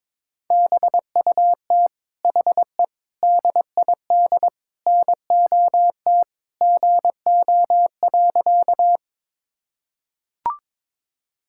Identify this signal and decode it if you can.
Morse code, heard as a keyed tone, about 22 wpm, 700 Hz. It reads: BUT HE DID NOT GO.
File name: practice / 22wpm / Sentences with 4-Word Phrases - 22wpm